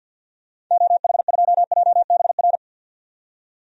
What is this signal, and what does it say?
Morse code, heard as a keyed tone, about 52 wpm, 700 Hz. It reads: OH2JBR
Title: DXCC-100 Call Signs - 6 Characters - 50wpm